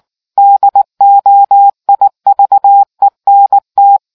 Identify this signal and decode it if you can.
Morse code, heard as a keyed tone, about 19 wpm, 800 Hz. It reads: DOIVENT